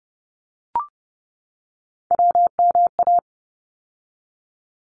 Morse code, heard as a keyed tone, about 30 wpm, 700 Hz. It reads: WMA
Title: ARRL Sections 30wpm